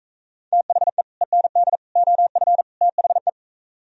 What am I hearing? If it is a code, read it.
Morse code, heard as a keyed tone, about 44 wpm, 700 Hz. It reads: THE END OF THE